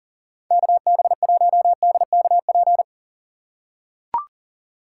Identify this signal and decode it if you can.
Morse code, heard as a keyed tone, about 40 wpm, 700 Hz. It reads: KB1DKP
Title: USA Call Signs - 6 Characters - 2x Character Spacing - 40wpm